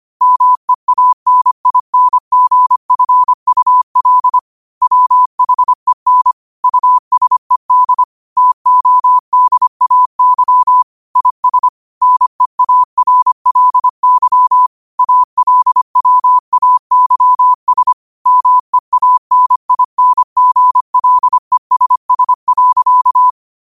Morse code, heard as a keyed tone, about 25 wpm, 1000 Hz. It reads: MEANINGFUL WHEN USED TODAY IS NEARLY ALWAYS MEANINGLESS.
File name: qotdfiles / 25wpm